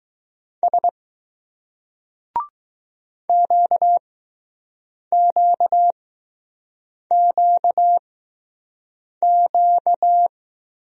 Morse code, heard as a keyed tone, about 20 wpm, spaced as for 6 wpm, 700 Hz. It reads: S QQQQ